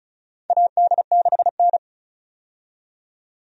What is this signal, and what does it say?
Morse code, heard as a keyed tone, about 34 wpm, 700 Hz. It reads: AD6N
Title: USA Call Signs - 4 Characters - 35wpm